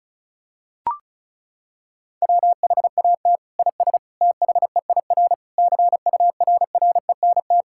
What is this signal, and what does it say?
Morse code, heard as a keyed tone, about 36 wpm, 700 Hz. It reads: WHAT IS THEIR CURRENT
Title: Sentences from Top 1000 Words - Review Part 6 - 35wpm